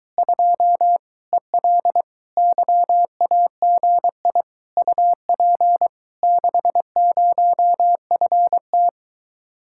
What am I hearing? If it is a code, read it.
Morse code, heard as a keyed tone, about 23 wpm, 700 Hz. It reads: 2 EL YAGI UP 60FT